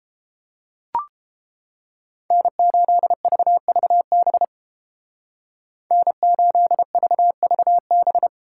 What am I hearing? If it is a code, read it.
Morse code, heard as a keyed tone, about 28 wpm, 700 Hz. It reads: N8VVB N8VVB